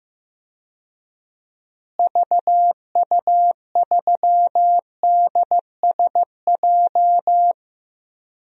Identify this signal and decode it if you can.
Morse code, heard as a keyed tone, about 16 wpm, 700 Hz. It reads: VU3DSJ
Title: DXCC-100 Call Signs - 6 Characters - Repeated 3x - 15wpm